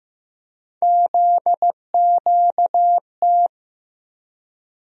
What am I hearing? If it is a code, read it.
Morse code, heard as a keyed tone, about 15 wpm, 700 Hz. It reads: ZQT
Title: USA Call Sign Postfixes - 3 Letters 15wpm